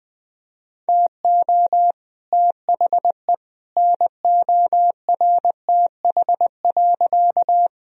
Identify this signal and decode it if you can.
Morse code, heard as a keyed tone, about 21 wpm, 700 Hz. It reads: TO THE NORTH.
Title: Sentences from Top 1000 Words - Review Part 1 - 20wpm